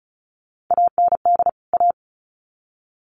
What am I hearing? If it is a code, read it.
Morse code, heard as a keyed tone, about 34 wpm, 700 Hz. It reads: AND A
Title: Sets of 2 Words 35wpm